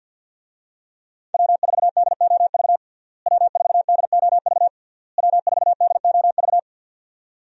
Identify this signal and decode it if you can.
Morse code, heard as a keyed tone, about 50 wpm, 700 Hz. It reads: W4DOV W4DOV W4DOV
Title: USA Call Signs - 5 Characters - Repeated 3x - 50wpm